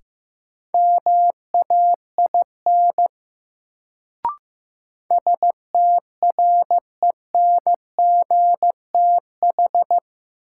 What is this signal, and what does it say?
Morse code, heard as a keyed tone, about 15 wpm, 700 Hz. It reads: MAIN STRENGTH
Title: Sentences from Top 1300 Words - New Word Drill Alt - 15wpm